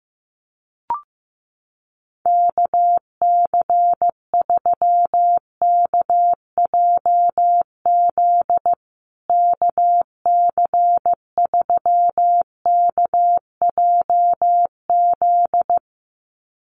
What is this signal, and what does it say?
Morse code, heard as a keyed tone, about 15 wpm, 700 Hz. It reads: KC3KJZ KC3KJZ
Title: USA Call Signs Repeated 2x 15wpm